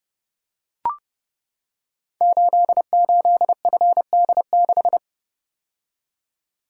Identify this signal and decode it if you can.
Morse code, heard as a keyed tone, about 30 wpm, 700 Hz. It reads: 88FD6
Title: Code Groups - 5 Letter:Numbers - 30wpm